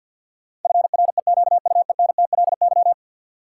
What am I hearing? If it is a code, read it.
Morse code, heard as a keyed tone, about 52 wpm, 700 Hz. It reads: FREQUENTLY
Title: Sentences from Top 1300 Words - New Word Drill - 50wpm